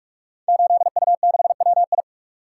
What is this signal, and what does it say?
Morse code, heard as a keyed tone, about 44 wpm, 700 Hz. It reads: 8U6WI
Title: Code Groups - 5 Letter:Numbers - 45wpm